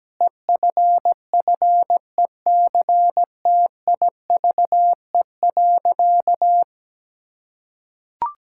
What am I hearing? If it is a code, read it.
Morse code, heard as a keyed tone, about 17 wpm, 700 Hz. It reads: EFFECTIVE.
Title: Sentences from Top 1000 Words - Review Part 6 - 17wpm